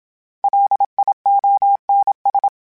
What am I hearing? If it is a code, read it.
Morse code, heard as a keyed tone, about 27 wpm, 800 Hz. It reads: LIONS
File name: Misc / Misc-mp3s / Word Pairs @25